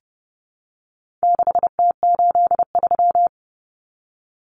Morse code, heard as a keyed tone, about 30 wpm, 700 Hz. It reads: 6T83
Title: Common Cut Numbers - 4-Digit 30wpm